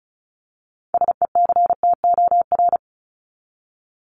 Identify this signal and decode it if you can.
Morse code, heard as a keyed tone, about 34 wpm, 700 Hz. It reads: SECTOR
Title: Top 1751-2000 Words 35wpm